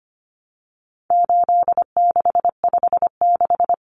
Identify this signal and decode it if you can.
Morse code, heard as a keyed tone, about 25 wpm, 700 Hz. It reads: 8656
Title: Common Cut Numbers - 4-Digit 25wpm